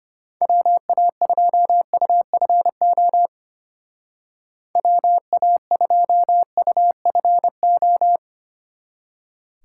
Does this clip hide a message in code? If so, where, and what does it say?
Yes: Morse code from 0.4 to 8.2 s, heard as a keyed tone, about 32 wpm, 700 Hz. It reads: WA2UFO WA2UFO